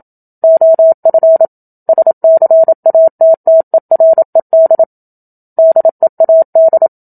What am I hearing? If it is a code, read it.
Morse code, heard as a keyed tone, about 27 wpm, 650 Hz. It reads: OF SCATTERED DEAD
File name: Book Two / 01-Chapter 1 - 27 WPM